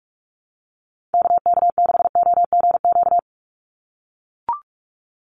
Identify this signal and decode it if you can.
Morse code, heard as a keyed tone, about 44 wpm, 700 Hz. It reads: KK6OGQ
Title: USA Call Signs 45wpm